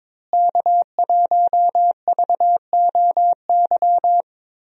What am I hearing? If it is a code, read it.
Morse code, heard as a keyed tone, about 22 wpm, 700 Hz. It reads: K1VOY
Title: USA Call Signs - 5 Characters - Repeated 2x - 22wpm